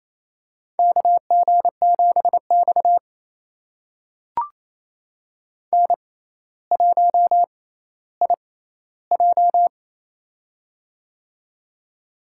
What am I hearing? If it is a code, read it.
Morse code, heard as a keyed tone, about 28 wpm, spaced as for 14 wpm, 700 Hz. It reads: KG7X N1IJ
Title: USA Call Signs - 4 Characters - 6x Character Spacing - 28wpm